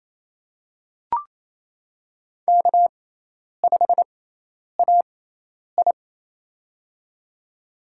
Morse code, heard as a keyed tone, about 28 wpm, spaced as for 10 wpm, 700 Hz. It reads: K5AI
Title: USA Call Signs - 4 Characters - 6x Character Spacing - 28wpm